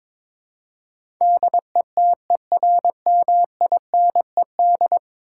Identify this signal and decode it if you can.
Morse code, heard as a keyed tone, about 22 wpm, 700 Hz. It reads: DETERMINED
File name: practice / 22wpm / 10 Letter Words - First 3 Letters Given - 22wpm